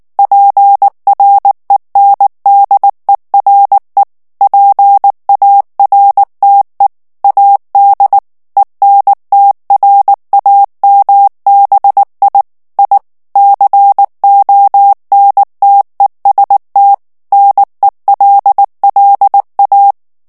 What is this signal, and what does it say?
Morse code, heard as a keyed tone, about 19 wpm, 800 Hz. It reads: PRENDERE PARTE AD ENTRAMBI I CONTEST NELLA
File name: morse mp3 / ITALIEN texte clair 2 vitesse 18 mots minute